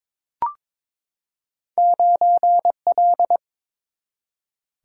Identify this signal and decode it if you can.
Morse code, heard as a keyed tone, about 22 wpm, 700 Hz. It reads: 9 L
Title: Single-Letter Single-Number 22wpm